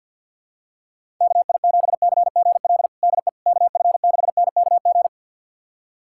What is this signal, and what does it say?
Morse code, heard as a keyed tone, about 53 wpm, 700 Hz. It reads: KI7CGL DE KF6NKG
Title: USA Callsign DE Callsign - 6 Characters - 50wpm